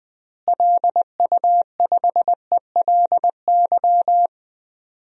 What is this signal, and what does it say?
Morse code, heard as a keyed tone, about 20 wpm, 700 Hz. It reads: LU5ELY
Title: DXCC-100 Call Signs - 6 Characters - Repeated 2x - 20wpm